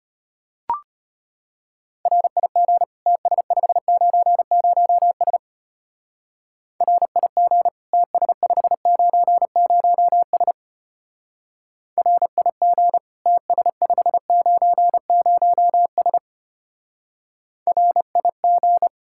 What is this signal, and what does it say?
Morse code, heard as a keyed tone, about 31 wpm, 700 Hz. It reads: RIG TS590S RIG TS590S RIG TS590S RIG
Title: Speed-Racing Common QSO Phrases Encore - 25wpm